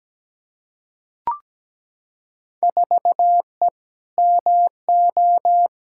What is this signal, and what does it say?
Morse code, heard as a keyed tone, about 17 wpm, 700 Hz. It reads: 4E MO
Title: Field Day Exchanges Encore - 17wpm